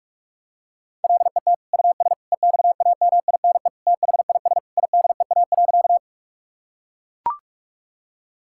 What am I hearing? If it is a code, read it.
Morse code, heard as a keyed tone, about 41 wpm, 700 Hz. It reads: LET US EXAMINE THIS IDEA.
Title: Sentences from Top 1600 Words 45wpm